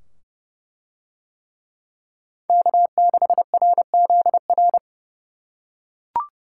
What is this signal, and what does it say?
Morse code, heard as a keyed tone, about 28 wpm, 700 Hz. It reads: K6RZR